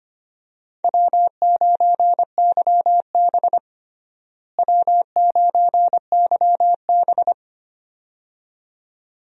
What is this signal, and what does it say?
Morse code, heard as a keyed tone, about 25 wpm, 700 Hz. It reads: W9YB W9YB
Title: USA Call Signs - 4 Characters - Repeated 2x - 25wpm